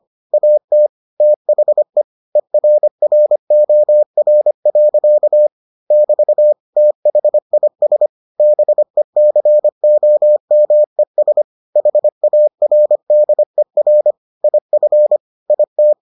Morse code, heard as a keyed tone, about 25 wpm, 600 Hz. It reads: AT THE ERROR. <BT> THIS BECOMES HARDER IF IT